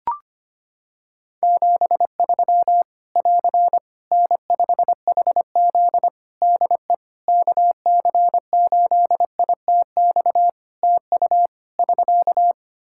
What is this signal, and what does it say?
Morse code, heard as a keyed tone, about 25 wpm, 700 Hz. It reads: 73 <AR> N5HZ DE KC8ITX TU <SK>
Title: Practice QSO - Exchange 4 - Ending - 25wpm